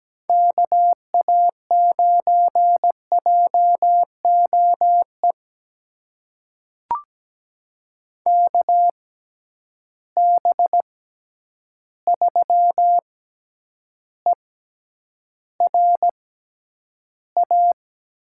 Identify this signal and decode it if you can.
Morse code, heard as a keyed tone, about 17 wpm, spaced as for 9 wpm, 700 Hz. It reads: KA9JOE KB3ERA
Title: USA Call Signs - 6x Character Spacing - 17wpm